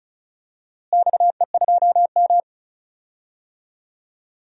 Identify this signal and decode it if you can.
Morse code, heard as a keyed tone, about 36 wpm, 700 Hz. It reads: XE2M
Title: DXCC-100 Call Signs - 4 Characters - Repeated 3x - 35wpm